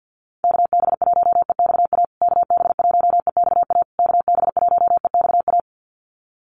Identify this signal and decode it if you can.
Morse code, heard as a keyed tone, about 50 wpm, 700 Hz. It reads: KB1EXA KB1EXA KB1EXA
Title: USA Call Signs Repeated 3x 50wpm